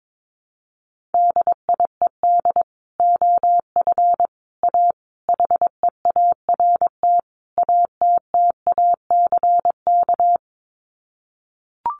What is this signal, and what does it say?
Morse code, heard as a keyed tone, about 22 wpm, 700 Hz. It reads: DIED OF A HEART ATTACK
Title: Sets of 5 Words Encore2 22wpm